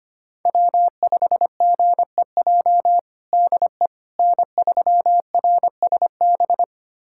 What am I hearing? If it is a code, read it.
Morse code, heard as a keyed tone, about 25 wpm, 700 Hz. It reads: W5GEJ DE N3RSB
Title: USA Callsign DE Callsign - 5 Characters - 25wpm